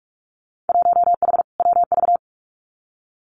Morse code, heard as a keyed tone, about 46 wpm, 700 Hz. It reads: 1H WV
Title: Winter Field Day Exchanges 45wpm